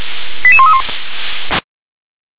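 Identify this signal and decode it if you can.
Five-tone selective call (ZVEI), heard as quick runs of five tones, tones of about 70 ms. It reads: 88131
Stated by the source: ZVEI five-tone call received on an SDR.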